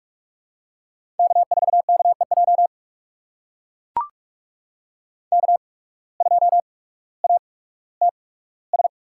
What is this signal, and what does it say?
Morse code, heard as a keyed tone, about 43 wpm, spaced as for 19 wpm, 700 Hz. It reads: K4KEJ K2ATS
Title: USA Call Signs - 5 Characters - 8x Character Spacing - 45wpm